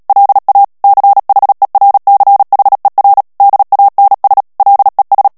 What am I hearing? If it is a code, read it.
Morse code, heard as a keyed tone, about 37 wpm, 800 Hz. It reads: LA CHERCHER DANS LES
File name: CW / FICHIERS-MP3 / FRANCAIS TEXTE CLAIR 2 VITESSE 35 MOTS MINUTES